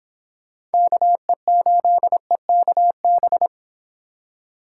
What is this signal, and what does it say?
Morse code, heard as a keyed tone, about 30 wpm, 700 Hz. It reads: KE8EKB